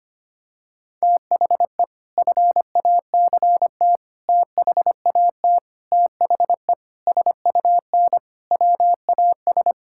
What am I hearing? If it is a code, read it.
Morse code, heard as a keyed tone, about 25 wpm, 700 Hz. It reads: THE FACT THAT THE SUN WAS
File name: practice / 25wpm / Sentences with 4-Word Phrases - 25wpm